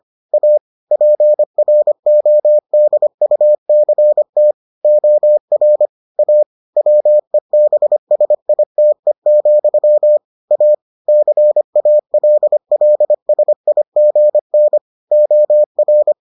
Morse code, heard as a keyed tone, about 25 wpm, 600 Hz. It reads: A PRODUCT OR A WEBSITE, A CALLSIGN OR